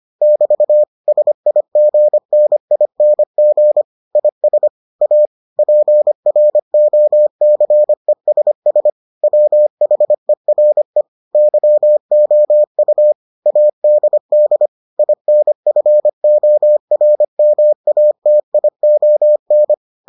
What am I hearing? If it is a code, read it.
Morse code, heard as a keyed tone, about 25 wpm, 600 Hz. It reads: <BT> SIGNING IS A PROCESS WHERE YOU ADD INFORMATION